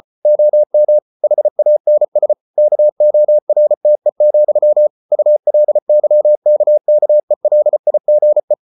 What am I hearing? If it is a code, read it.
Morse code, heard as a keyed tone, about 34 wpm, 600 Hz. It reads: OM HANS KORTE, ULYKKELIGE
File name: Ferie QRQ 34wpm